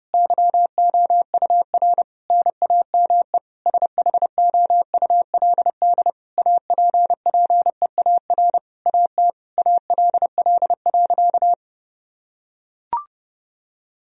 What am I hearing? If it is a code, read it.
Morse code, heard as a keyed tone, about 30 wpm, 700 Hz. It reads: YOUR NAME SHOULD APPEAR AT ALL.
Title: Sentences from Top 1000 Words - Review Part 2 - 30wpm